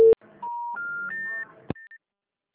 Special information tone.
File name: out- -2016- - - .96444